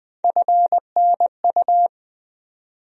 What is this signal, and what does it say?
Morse code, heard as a keyed tone, about 20 wpm, 700 Hz. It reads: FNU
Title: USA Call Sign Postfixes - 3 Letters 20wpm